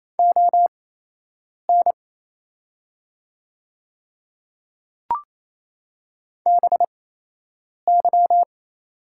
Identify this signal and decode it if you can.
Morse code, heard as a keyed tone, about 28 wpm, spaced as for 8 wpm, 700 Hz. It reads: ON BY
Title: 8x Character Spacing - Top 100 Words - 28wpm